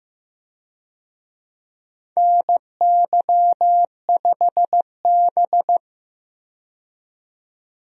Morse code, heard as a keyed tone, about 15 wpm, 700 Hz. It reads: NY5B
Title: USA Call Signs - 4 Characters - 15wpm